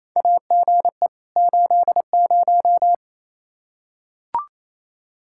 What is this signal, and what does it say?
Morse code, heard as a keyed tone, about 28 wpm, 700 Hz. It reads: AGE 80
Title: K1USN SST QSO Practice 28wpm